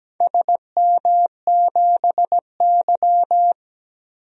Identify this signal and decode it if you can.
Morse code, heard as a keyed tone, about 17 wpm, 700 Hz. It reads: SM7Y
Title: DXCC-100 Call Signs - 4 Characters - Repeated 2x - 17wpm